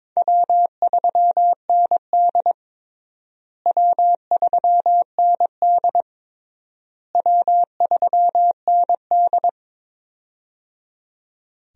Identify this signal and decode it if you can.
Morse code, heard as a keyed tone, about 21 wpm, 700 Hz. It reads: W3ND W3ND W3ND
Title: USA Call Signs - 4 Characters - Repeated 3x - 22wpm